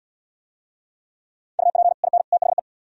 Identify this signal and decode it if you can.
Morse code, heard as a keyed tone, about 70 wpm, 700 Hz. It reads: UP IN THE